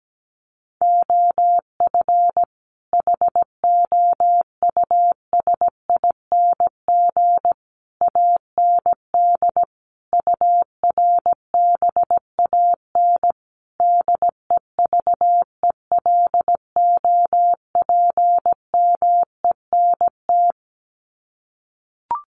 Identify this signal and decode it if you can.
Morse code, heard as a keyed tone, about 17 wpm, 700 Hz. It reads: OF HOUSING AND URBAN DEVELOPMENT
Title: Sets of 5 Words Encore2 17wpm